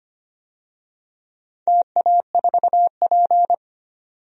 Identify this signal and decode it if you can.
Morse code, heard as a keyed tone, about 25 wpm, 700 Hz. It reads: TA4P
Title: DXCC-100 Call Signs - 4 Characters - 25wpm